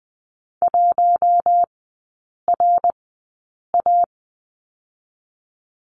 Morse code, heard as a keyed tone, about 20 wpm, spaced as for 8 wpm, 700 Hz. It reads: 1RA
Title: 081 Lesson 14e - Number 2 - Three Characters 20wpm